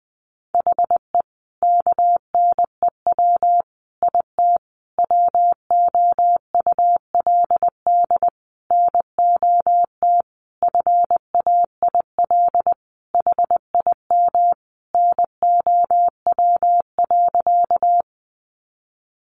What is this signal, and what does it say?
Morse code, heard as a keyed tone, about 20 wpm, 700 Hz. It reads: HE KNEW IT WOULD NOT FAIL HIM NOW.
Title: Sentences from Top 1100 Words 20wpm